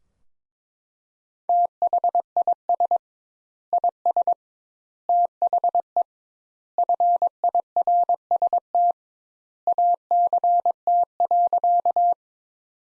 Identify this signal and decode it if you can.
Morse code, heard as a keyed tone, about 22 wpm, 700 Hz. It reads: THIS IS THE FIRST ACT.
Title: Sentences Top 200 Words - 22wpm